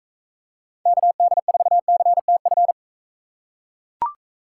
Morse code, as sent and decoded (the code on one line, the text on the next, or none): -.- -.. ....- -.-. - ..-.
KD4CTF